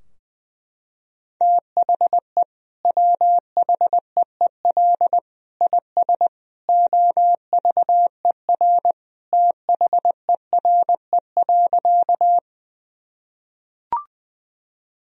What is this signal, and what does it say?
Morse code, heard as a keyed tone, about 20 wpm, 700 Hz. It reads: THE WHEEL IS OVER THERE.
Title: Sentences from Top 1000 Words - Review Part 3 - 20wpm